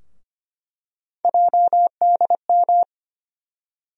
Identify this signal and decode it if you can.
Morse code, heard as a keyed tone, about 25 wpm, 700 Hz. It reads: JDM
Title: USA Call Sign Postfixes - 3 Letters 25wpm